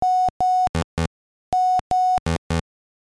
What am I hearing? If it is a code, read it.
There is no code.